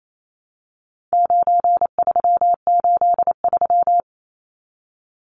Morse code, heard as a keyed tone, about 28 wpm, 700 Hz. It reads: 9383
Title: Numbers - 4-Digit 28wpm